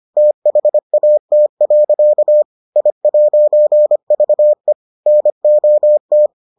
Morse code, heard as a keyed tone, about 25 wpm, 600 Hz. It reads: THAT. I'VE NOT